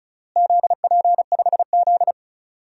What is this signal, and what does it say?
Morse code, heard as a keyed tone, about 36 wpm, 700 Hz. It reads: ZP5Z
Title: DXCC-100 Call Signs - 4 Characters - Repeated 3x - 35wpm